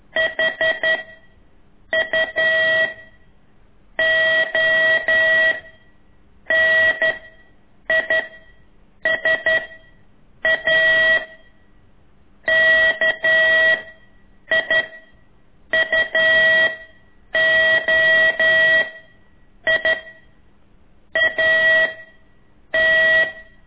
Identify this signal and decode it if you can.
Morse code, heard as a keyed tone, about 11 wpm, 1950 Hz. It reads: H U O N I S A K I U O I A T
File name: ik1weg2 / CORSI CW